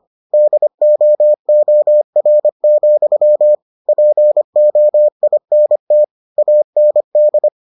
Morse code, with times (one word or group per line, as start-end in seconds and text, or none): 0.3-3.5 DOOR,
3.9-6.0 POINT
6.4-7.5 AND